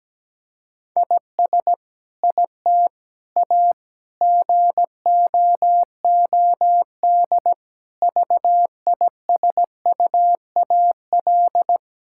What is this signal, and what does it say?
Morse code, heard as a keyed tone, about 17 wpm, 700 Hz. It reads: IS IT A GOOD VISUAL
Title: Sentences from Top 1000 Words - Review Part 6 - 17wpm